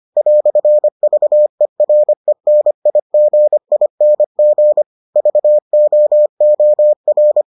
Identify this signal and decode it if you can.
Morse code, heard as a keyed tone, about 25 wpm, 600 Hz. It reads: "VERENIGING VOOR